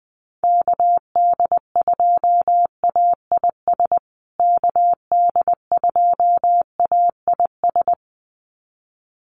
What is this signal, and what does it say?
Morse code, heard as a keyed tone, about 20 wpm, 700 Hz. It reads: KD2AIS KD2AIS
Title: USA Call Signs Repeated 2x 20wpm